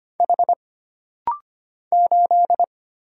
Morse code, heard as a keyed tone, about 25 wpm, 700 Hz. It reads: H 8
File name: Single Character (Letter:Number:Punctuation:Prosign) - Rapid-Fire - 25wpm